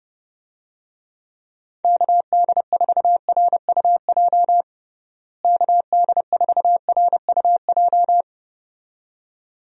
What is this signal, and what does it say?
Morse code, heard as a keyed tone, about 30 wpm, 700 Hz. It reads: KD4RUJ KD4RUJ
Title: USA Call Signs - 6 Characters - Repeated 2x - 30wpm